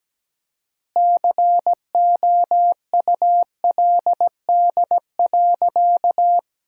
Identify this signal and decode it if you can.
Morse code, heard as a keyed tone, about 17 wpm, 700 Hz. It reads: COULD.
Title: 4x Word-Spacing - Sentences from Top 500 Words - 17wpm